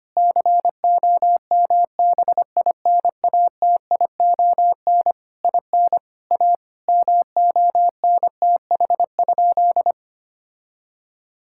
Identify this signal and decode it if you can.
Morse code, heard as a keyed tone, about 25 wpm, 700 Hz. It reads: COMBINATION IN A MONTH?
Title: Sentences from Top 1000 Words - Review Part 6 - 25wpm